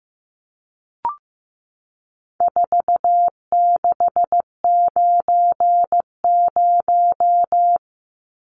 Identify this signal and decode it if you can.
Morse code, heard as a keyed tone, about 15 wpm, 700 Hz. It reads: 4690
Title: Numbers - 4-Digit 15wpm